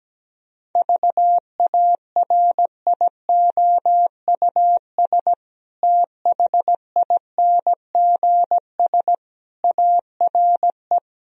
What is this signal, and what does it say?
Morse code, heard as a keyed tone, about 17 wpm, 700 Hz. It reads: VARIOUS THINGS ARE
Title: Sentences from Top 1000 Words - Review Part 4 - 17wpm